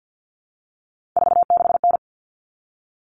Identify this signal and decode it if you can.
Morse code, heard as a keyed tone, about 50 wpm, 700 Hz. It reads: 46N
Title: All Cut Numbers - 3-Digit 50wpm